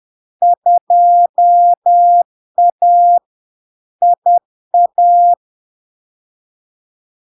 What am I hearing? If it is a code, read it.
Morse code, heard as a keyed tone, about 10 wpm, 700 Hz. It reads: 2A IA
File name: LARK-FD-2022-Log-in-CW-10wpm-700-Hz